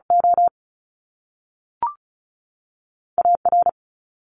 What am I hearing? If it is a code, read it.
Morse code, heard as a keyed tone, about 36 wpm, 700 Hz. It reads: O A R
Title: Top Two Letter Combinations in Words 35wpm